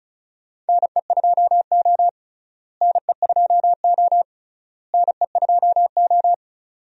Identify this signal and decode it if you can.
Morse code, heard as a keyed tone, about 36 wpm, 700 Hz. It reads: NE2O NE2O NE2O
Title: USA Call Signs - 4 Characters - Repeated 3x - 35wpm